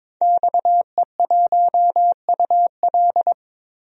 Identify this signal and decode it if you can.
Morse code, heard as a keyed tone, about 22 wpm, 700 Hz. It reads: XE1UL
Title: DXCC-100 Call Signs - 5 Characters - 22wpm